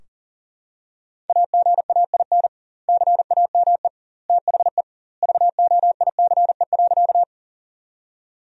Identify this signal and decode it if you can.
Morse code, heard as a keyed tone, about 37 wpm, 700 Hz. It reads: AGAIN CAME THE VOICE.
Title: 2x Word-Spacing - Sentences from Top 500 Words - 40wpm